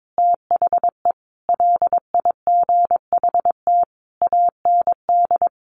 Morse code, heard as a keyed tone, about 22 wpm, 700 Hz. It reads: THE LIGHT AND
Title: Sentences from Top 1300 Words 22wpm